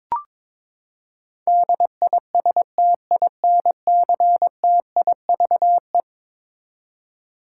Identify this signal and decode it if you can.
Morse code, heard as a keyed tone, about 22 wpm, 700 Hz. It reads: DISTINCTIVE
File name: practice / 22wpm / Eleven Letter Words - 22wpm